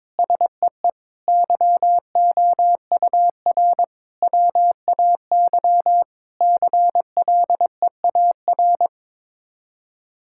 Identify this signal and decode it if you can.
Morse code, heard as a keyed tone, about 22 wpm, 700 Hz. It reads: SEE YOUR WAY CLEAR
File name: practice / 22wpm / Sets of 8 Words - 22wpm